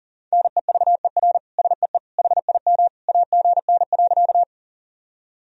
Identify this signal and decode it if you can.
Morse code, heard as a keyed tone, about 40 wpm, 700 Hz. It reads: NEVER SEE HIM AGN.
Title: Sentences from Top 300 Words and Abbreviations 40wpm